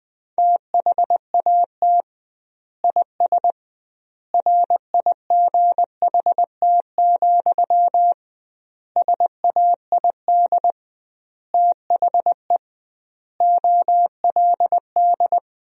Morse code, heard as a keyed tone, about 20 wpm, 700 Hz. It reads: THAT IS RIGHT, SAID THE OLD
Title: 2x Word-Spacing - Sentences from Top 200 Words - 20wpm